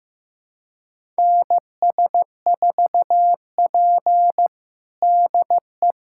Morse code, heard as a keyed tone, about 15 wpm, 700 Hz. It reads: NS4P DE
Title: USA Callsign DE Callsign - 4 Characters - 15wpm